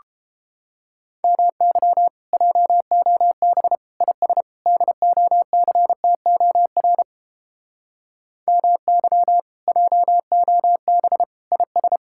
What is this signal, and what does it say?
Morse code, heard as a keyed tone, about 31 wpm, 700 Hz. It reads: MY JOB IS DOCTOR MY JOB IS